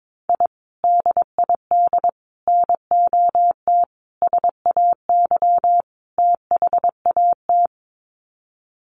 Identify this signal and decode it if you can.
Morse code, heard as a keyed tone, about 22 wpm, 700 Hz. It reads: I DID NOT SAY THAT
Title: Sets of 5 Words Encore2 22wpm